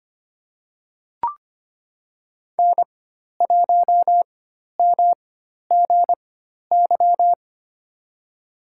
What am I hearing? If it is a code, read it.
Morse code, heard as a keyed tone, about 25 wpm, spaced as for 12 wpm, 700 Hz. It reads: N1MGY
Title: USA Call Signs - 4x Character Spacing - 25wpm